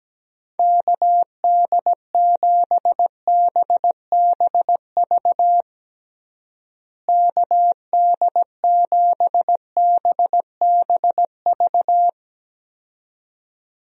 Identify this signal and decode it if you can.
Morse code, heard as a keyed tone, about 17 wpm, 700 Hz. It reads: KD7BBV KD7BBV
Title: USA Call Signs - 6 Characters - Repeated 2x - 17wpm